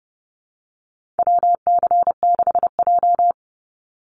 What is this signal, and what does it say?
Morse code, heard as a keyed tone, about 30 wpm, 700 Hz. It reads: WC6J
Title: NA SOTA Chaser Callsigns 30wpm